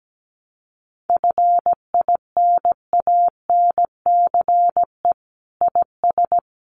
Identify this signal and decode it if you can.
Morse code, heard as a keyed tone, about 17 wpm, 700 Hz. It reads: FINANCE IS
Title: Sentences from Top 1500 Words 17wpm